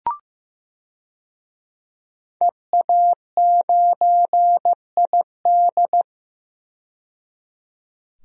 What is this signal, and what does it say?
Morse code, heard as a keyed tone, about 15 wpm, 700 Hz. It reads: EA9ID
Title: DXCC-100 Call Signs - 5 Characters - 15wpm